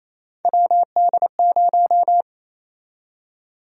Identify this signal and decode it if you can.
Morse code, heard as a keyed tone, about 28 wpm, 700 Hz. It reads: WD0
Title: USA Call Sign Prefixes - Repeated 2x - 28wpm